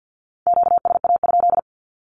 Morse code, heard as a keyed tone, about 50 wpm, 700 Hz. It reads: QSA?
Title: Q-Codes 50wpm